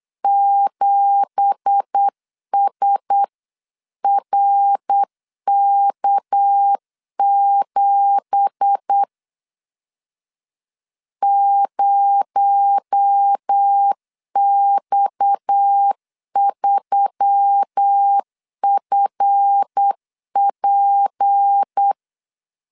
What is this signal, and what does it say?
Morse code, heard as a keyed tone, about 8 wpm, 800 Hz. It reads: 7SRK7 0X3FP
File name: CW / DET / 1 9 1500C S C 8wpm 5653